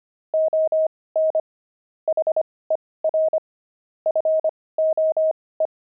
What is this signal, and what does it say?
Morse code, heard as a keyed tone, about 25 wpm, spaced as for 18 wpm, 630 Hz. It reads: ON HER FOE
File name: pangram-fs-18-cs-25-barking